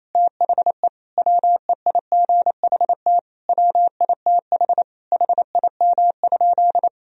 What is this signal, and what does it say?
Morse code, heard as a keyed tone, about 28 wpm, 700 Hz. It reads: THE WEIGHT WITH HIM?
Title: Sentences from Top 1000 Words - Review Part 3 - 28wpm